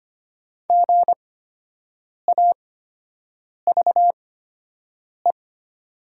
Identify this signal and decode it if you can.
Morse code, heard as a keyed tone, about 25 wpm, spaced as for 7 wpm, 700 Hz. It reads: GAVE
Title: 8x Character Spacing - Top 500 Words - 25wpm